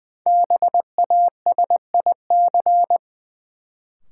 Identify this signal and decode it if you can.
Morse code, heard as a keyed tone, about 20 wpm, 700 Hz. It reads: BASIC
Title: Sentences from Top 1000 Words - Review Part 3 - New Word Drill - 20wpm